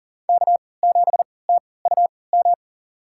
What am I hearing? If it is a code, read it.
Morse code, heard as a keyed tone, about 40 wpm, spaced as for 23 wpm, 700 Hz. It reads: K7TUM
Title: USA Call Signs - 5 Characters - 3x Character Spacing - 40wpm